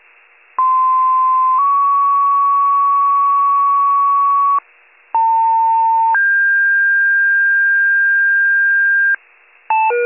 Two-tone page, heard as a one-second tone, then a three-second tone, about 1040 then 1120 Hz, about 900 then 1600 Hz. It is heard from 0.6 to 9.2 s.